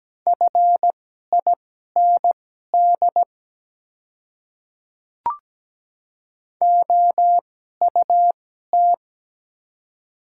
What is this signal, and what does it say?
Morse code, heard as a keyed tone, about 17 wpm, spaced as for 12 wpm, 700 Hz. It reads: FIND OUT
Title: 2x Character Spacing - Top 100 Words - 17wpm